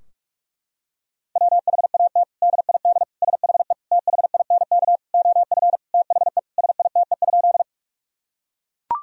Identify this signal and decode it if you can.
Morse code, heard as a keyed tone, about 41 wpm, 700 Hz. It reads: WHAT DID SHE THINK OF THE SITE?
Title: Sentences from Top 1000 Words - Review Part 4 - 45wpm